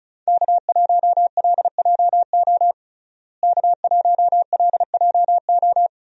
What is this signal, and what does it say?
Morse code, heard as a keyed tone, about 34 wpm, 700 Hz. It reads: K1LJO K1LJO
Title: USA Call Signs - 5 Characters - Repeated 2x - 35wpm